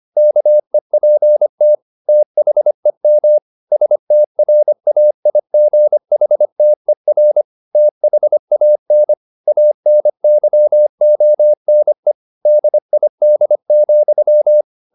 Morse code, heard as a keyed tone, about 25 wpm, 600 Hz. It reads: KEPT THEM STRAIGHTER THAN ANYONE DID,
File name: the old man and the sea part 1 25wpm